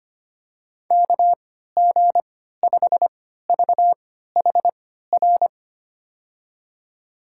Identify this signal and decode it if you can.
Morse code, heard as a keyed tone, about 25 wpm, spaced as for 14 wpm, 700 Hz. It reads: KG5VHR
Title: USA Call Signs - 6 Characters - 3x Character Spacing - 25wpm